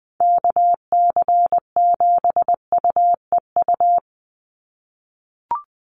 Morse code, heard as a keyed tone, about 20 wpm, 700 Hz. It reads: KC7UEU
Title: USA Call Signs 20wpm